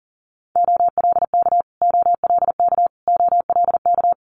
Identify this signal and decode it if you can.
Morse code, heard as a keyed tone, about 40 wpm, 690 Hz. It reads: OLK OLK OLK
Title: Multi-letter Phonemes Repeated 3x 40wpm